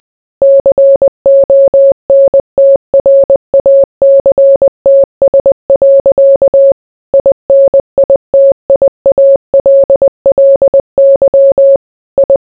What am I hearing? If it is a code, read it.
Morse code, heard as a keyed tone, about 20 wpm, 550 Hz. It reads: CONTRACTS. INITIALLY I